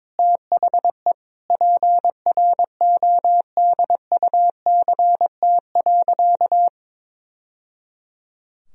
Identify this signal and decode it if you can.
Morse code, heard as a keyed tone, about 21 wpm, 700 Hz. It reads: THE PRODUCT.
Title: Sentences from Top 1000 Words - Review Part 2 - 22wpm